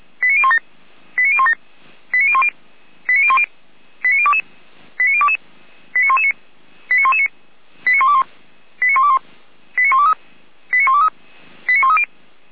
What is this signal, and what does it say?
Five-tone selective call (ZVEI), heard as quick runs of five tones, tones of about 70 ms. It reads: 89017 89017 89010 89010 89022 89022 89119 89119 89121 89121 89123 89123 89130